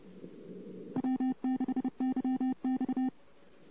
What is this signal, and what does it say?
Morse code, heard as a keyed tone, about 30 wpm, 260 Hz. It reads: W6YX
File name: CW / 2008-ss-cw-k5tr-op-n5rz -14:00